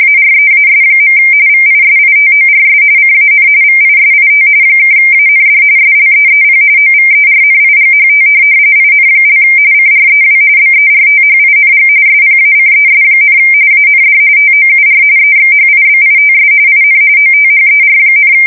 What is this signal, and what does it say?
RTTY, about 45.45 baud. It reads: AREAS. VISITORS CAN VISIT (AND SLEEP OVERNIGHT IN) CREW BERTHING SPACES, AND TOUR EXHIBITS IN OTHER,
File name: fsk170 / USS-New-Jersey